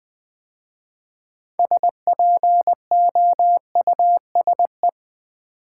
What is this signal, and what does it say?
Morse code, heard as a keyed tone, about 20 wpm, 700 Hz. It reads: SPOUSE
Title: 6 Letter Words - First 3 Letters Given - 20wpm